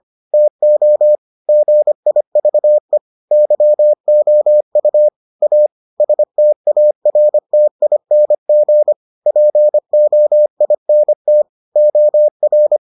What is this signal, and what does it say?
Morse code, heard as a keyed tone, about 25 wpm, 600 Hz. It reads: TO GIVE YOU A STARTING POINT OR